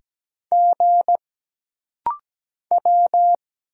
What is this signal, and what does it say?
Morse code, heard as a keyed tone, about 17 wpm, 700 Hz. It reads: G W
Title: Single Letters - Rapid-Fire 17wpm